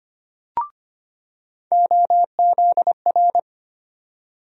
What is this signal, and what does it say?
Morse code, heard as a keyed tone, about 25 wpm, 700 Hz. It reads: OZR
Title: USA Call Sign Postfixes - 3 Letters 25wpm